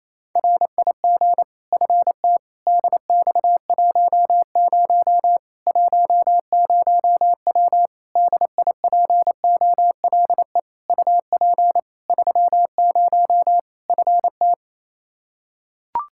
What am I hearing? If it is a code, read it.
Morse code, heard as a keyed tone, about 27 wpm, 700 Hz. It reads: RIG FT DX10 10W DIPOLE UP 30 FT
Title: Practice QSO - Exchange 2 - Info - 28wpm